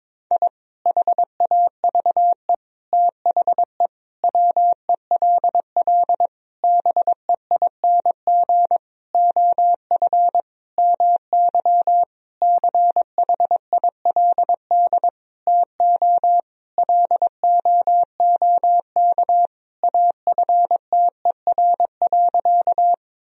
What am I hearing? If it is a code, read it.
Morse code, heard as a keyed tone, about 22 wpm, 700 Hz. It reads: I HAVE THE WELL BEING OF MY CHILD TO LOOK AFTER.
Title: Sentences from Top 1000 Words - Review Part 3 - 22wpm